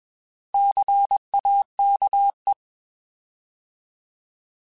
Morse code, heard as a keyed tone, about 21 wpm, 800 Hz. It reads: CAKE
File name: Alphabetical / Alphabetical mp3s / Cs @20